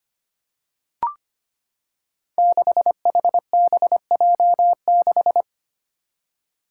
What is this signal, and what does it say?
Morse code, heard as a keyed tone, about 25 wpm, 700 Hz. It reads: 6HBJ6
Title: Five Challenging Letter-Number Combinations 25wpm